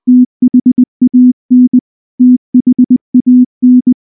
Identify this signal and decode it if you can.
Morse code, heard as a keyed tone, about 20 wpm, 250 Hz. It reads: THAN THAN
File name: morse / morse-training-20WPM-6